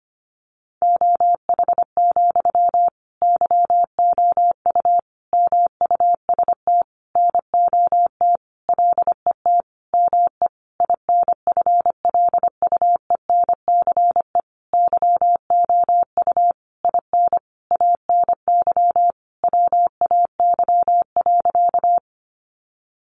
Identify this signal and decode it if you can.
Morse code, heard as a keyed tone, about 25 wpm, 700 Hz. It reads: OH, YOU MUST NOT LET ME INFLUENCE YOU IN ANY WAY.
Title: Sentences from Top 1100 Words 25wpm